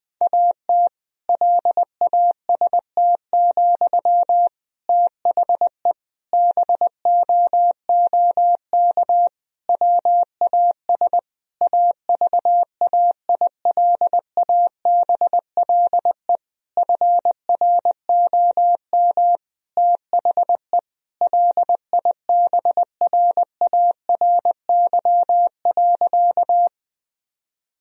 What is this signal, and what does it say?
Morse code, heard as a keyed tone, about 21 wpm, 700 Hz. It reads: AT LAST, THE BOOK WAS AVAILABLE FROM THE LIBRARY.
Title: Sentences from Top 1000 Words - Review Part 3 - 20wpm